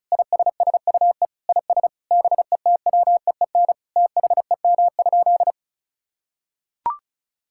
Morse code, heard as a keyed tone, about 36 wpm, 700 Hz. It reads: ISSUE IS BETWEEN THEM?
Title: Sentences from Top 1000 Words - Review Part 6 - 35wpm